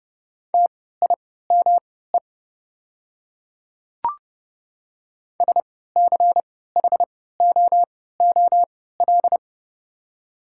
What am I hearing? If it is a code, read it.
Morse code, heard as a keyed tone, about 30 wpm, spaced as for 17 wpm, 700 Hz. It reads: TIME SCHOOL
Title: 3x Character Spacing - Top 300 Words - 30wpm